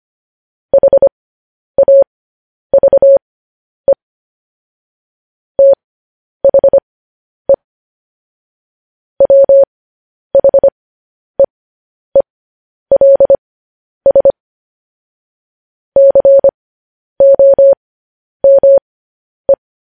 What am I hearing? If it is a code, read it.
Morse code, heard as a keyed tone, about 25 wpm, spaced as for 10 wpm, 550 Hz. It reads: HAVE THE WHEELS COME
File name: podcast / News headlines-1x-25x10